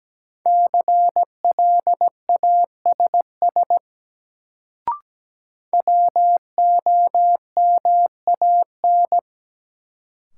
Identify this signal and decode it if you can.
Morse code, heard as a keyed tone, about 17 wpm, 700 Hz. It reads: CLASS WOMAN
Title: Sentences from Top 1000 Words - Review Part 2 - New Word Drill - 17wpm